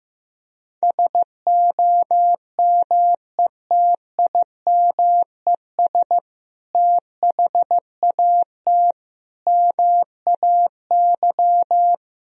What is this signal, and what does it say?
Morse code, heard as a keyed tone, about 15 wpm, 700 Hz. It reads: SOMETIMES THAT MAY
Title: Sentences from Top 1000 Words - Review Part 3 - 15wpm